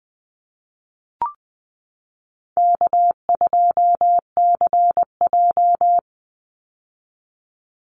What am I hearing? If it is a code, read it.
Morse code, heard as a keyed tone, about 19 wpm, 700 Hz. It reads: K2CJ
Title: Call Signs - 20wpm